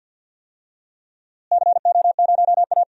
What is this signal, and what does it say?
Morse code, heard as a keyed tone, about 52 wpm, 700 Hz. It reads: CO0A
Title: DXCC-100 Call Signs - 4 Characters - Repeated 2x - 50wpm